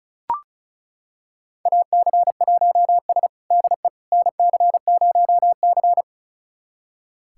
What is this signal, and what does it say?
Morse code, heard as a keyed tone, about 36 wpm, 700 Hz. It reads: AC1S DE NC0C
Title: USA Callsign DE Callsign - 4 Characters - 35wpm